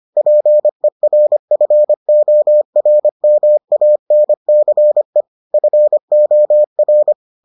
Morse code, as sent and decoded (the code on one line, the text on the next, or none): .--. .-. ..-. --- .-. -- .- -. -.-. / ..-. --- .-.
PERFORMANCE FOR